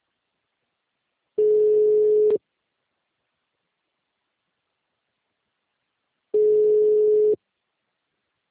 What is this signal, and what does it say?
Ringback tone (Europe, 425 Hz).
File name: out- -2019- - - .64993